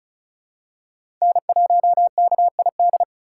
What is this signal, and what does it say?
Morse code, heard as a keyed tone, about 34 wpm, 700 Hz. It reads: N1KID
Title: USA Call Signs - 5 Characters - 3x Character Spacing - 35wpm